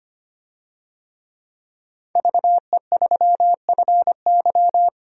Morse code, heard as a keyed tone, about 25 wpm, 700 Hz. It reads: VE3FY
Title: DXCC-100 Call Signs - 5 Characters - 25wpm